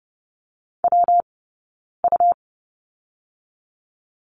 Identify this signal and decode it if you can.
Morse code, heard as a keyed tone, about 32 wpm, 700 Hz. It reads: W U
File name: Lesson 20d - Number 3 - Two Characters 30wpm